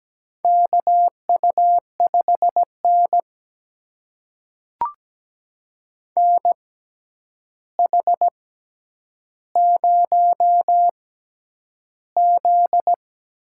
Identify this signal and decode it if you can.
Morse code, heard as a keyed tone, about 17 wpm, spaced as for 9 wpm, 700 Hz. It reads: KU5N NH0Z
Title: USA Call Signs - 4 Characters - 6x Character Spacing - 17wpm